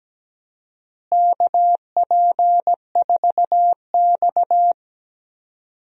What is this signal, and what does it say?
Morse code, heard as a keyed tone, about 17 wpm, 700 Hz. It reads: KP4X
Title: DXCC-100 Call Signs - 4 Characters - Repeated 3x - 17wpm